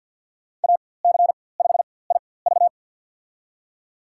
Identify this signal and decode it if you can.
Morse code, heard as a keyed tone, about 53 wpm, spaced as for 24 wpm, 700 Hz. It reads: AC5IV
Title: USA Call Signs - 5 Characters - 4x Character Spacing - 50wpm